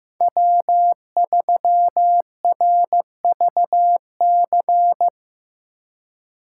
Morse code, heard as a keyed tone, about 18 wpm, 700 Hz. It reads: W3RVC
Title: Speed Racing - USA 1x3 Call Signs 15wpm